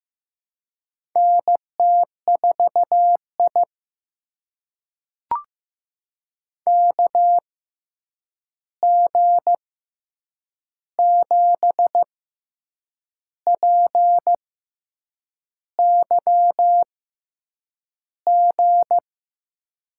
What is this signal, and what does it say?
Morse code, heard as a keyed tone, about 15 wpm, spaced as for 7 wpm, 700 Hz. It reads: NT4I KG7PYG
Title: USA Call Signs - 6x Character Spacing - 15wpm